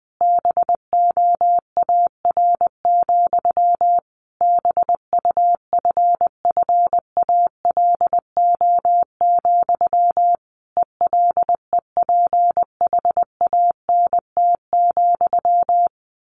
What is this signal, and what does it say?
Morse code, heard as a keyed tone, about 20 wpm, 700 Hz. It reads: BOAR, BUFFALO, ELEPHANT,
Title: A Sound Of Thunder 20wpm